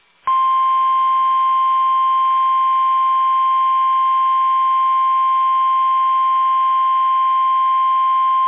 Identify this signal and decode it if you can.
Emergency Alert System (SAME) tones (the 1050 Hz attention tone). It is heard from 0.9 to 7.6 s.
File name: eas / KUSO / KUSO